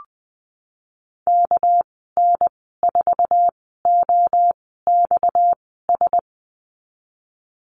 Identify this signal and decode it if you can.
Morse code, heard as a keyed tone, about 19 wpm, spaced as for 14 wpm, 700 Hz. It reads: KN4OXS
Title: 231 Lesson 37g - Letter x - Call Signs 20wpm